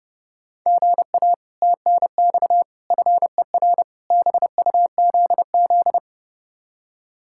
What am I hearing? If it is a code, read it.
Morse code, heard as a keyed tone, about 36 wpm, 700 Hz. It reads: GA TNX FER BUZZ
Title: Speed-Racing Common QSO Phrases Encore - 30wpm